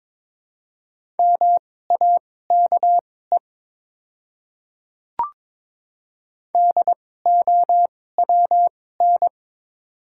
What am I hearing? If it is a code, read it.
Morse code, heard as a keyed tone, about 21 wpm, spaced as for 16 wpm, 690 Hz. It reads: MAKE DOWN
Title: 2x Character Spacing - Top 100 Words - 22wpm